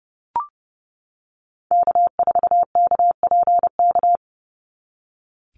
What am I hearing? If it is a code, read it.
Morse code, heard as a keyed tone, about 30 wpm, 700 Hz. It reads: K4KPK
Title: NA SOTA Activator Callsigns 30wpm